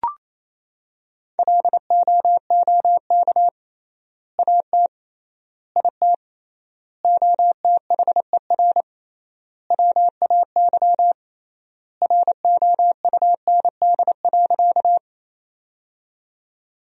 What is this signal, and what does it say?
Morse code, heard as a keyed tone, about 28 wpm, 700 Hz. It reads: LOOK AT IT OTHER WAY ROUND.
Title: 3x Word-Spacing - Sentences from Top 200 Words - 28wpm